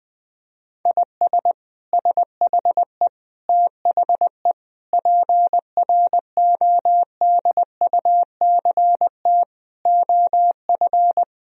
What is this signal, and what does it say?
Morse code, heard as a keyed tone, about 20 wpm, 700 Hz. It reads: IS SHE THE PRODUCT OF
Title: Sentences from Top 1000 Words - Review Part 2 - 20wpm